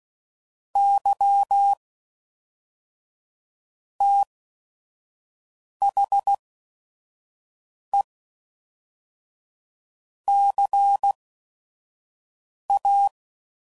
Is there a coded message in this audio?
Yes: Morse code from 0.8 to 13.1 s, heard as a keyed tone, about 16 wpm, 800 Hz. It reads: Y T H E C A